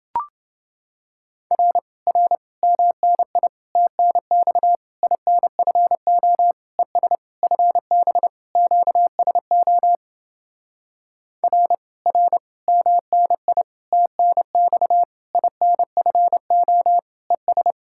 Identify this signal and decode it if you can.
Morse code, heard as a keyed tone, about 29 wpm, 700 Hz. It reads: R R MNI TNX INFO ES FB QSO R R MNI TNX INFO ES